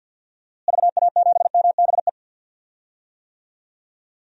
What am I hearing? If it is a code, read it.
Morse code, heard as a keyed tone, about 52 wpm, 700 Hz. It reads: VA7MBE
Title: DXCC-100 Call Signs - 6 Characters - Repeated 2x - 50wpm